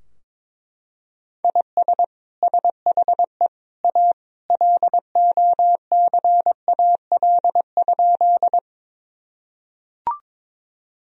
Morse code, heard as a keyed tone, about 21 wpm, 700 Hz. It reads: IS SHE A LOCAL?
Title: Sentences from Top 1000 Words - Review Part 3 - 22wpm